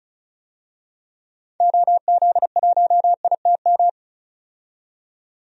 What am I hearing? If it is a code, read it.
Morse code, heard as a keyed tone, about 34 wpm, 700 Hz. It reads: OZ1ITM
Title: DXCC-100 Call Signs - 6 Characters - 35wpm